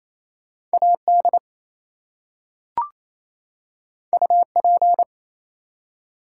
Morse code, heard as a keyed tone, about 27 wpm, 700 Hz. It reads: AD UP
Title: Two Letter Words - 28wpm